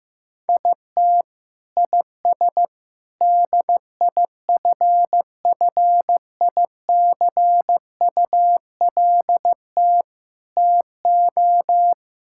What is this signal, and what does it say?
Morse code, heard as a keyed tone, about 15 wpm, 700 Hz. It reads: IT IS DIFFICULT TO
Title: Sentences from Top 1600 Words 15wpm